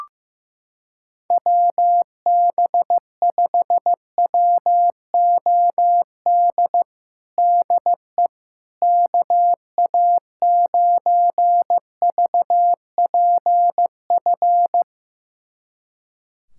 Morse code, heard as a keyed tone, about 15 wpm, 700 Hz. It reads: WB5WOD DE KA9VPF
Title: USA Callsign DE Callsign - 6 Characters - 15wpm